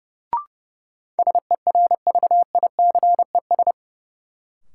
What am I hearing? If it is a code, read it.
Morse code, heard as a keyed tone, about 30 wpm, 700 Hz. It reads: SERVICES
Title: Sets of 5 Words - New Word Drill - 30wpm